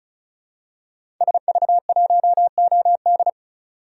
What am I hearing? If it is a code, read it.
Morse code, heard as a keyed tone, about 36 wpm, 700 Hz. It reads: SV1OD
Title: DXCC-100 Call Signs - 5 Characters - 35wpm